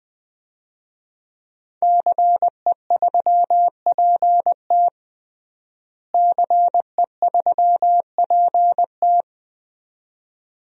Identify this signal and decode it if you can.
Morse code, heard as a keyed tone, about 21 wpm, 700 Hz. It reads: CE3PT CE3PT
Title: DXCC-100 Call Signs - 5 Characters - Repeated 2x - 20wpm